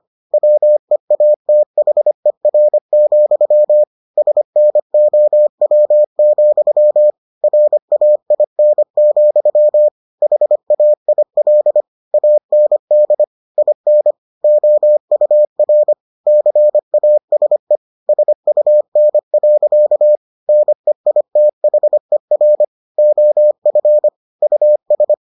Morse code, heard as a keyed tone, about 25 wpm, 600 Hz. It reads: WEATHER, SNOW, RAIN, HAIL AND IN OUR CASE SUN. NEITHER OF US